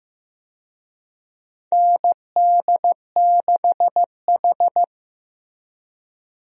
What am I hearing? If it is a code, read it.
Morse code, heard as a keyed tone, about 15 wpm, 700 Hz. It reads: ND6H
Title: USA Call Signs - 4 Characters - Repeated 2x - 15wpm